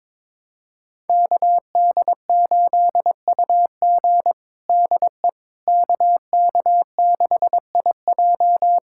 Morse code, heard as a keyed tone, about 22 wpm, 700 Hz. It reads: KD8UG DE KK6IJ
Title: USA Callsign DE Callsign - 5 Characters - 22wpm